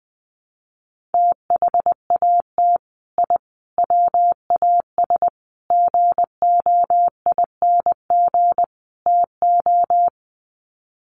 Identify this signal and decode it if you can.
Morse code, heard as a keyed tone, about 20 wpm, 700 Hz. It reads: THAT I WAS GOING TO
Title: Sets of 5 Words Encore 20wpm